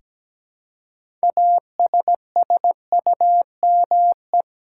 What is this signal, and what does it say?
Morse code, heard as a keyed tone, about 17 wpm, 700 Hz. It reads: ASSUME
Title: Six Letter Words - 17wpm